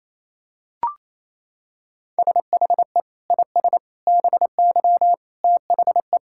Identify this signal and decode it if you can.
Morse code, heard as a keyed tone, about 28 wpm, 700 Hz. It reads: SHE IS BY THE
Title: Sentences from Top 1600 Words 28wpm